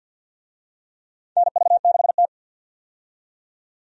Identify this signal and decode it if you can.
Morse code, heard as a keyed tone, about 50 wpm, 700 Hz. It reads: NV6T
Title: USA Call Signs - 4 Characters - 50wpm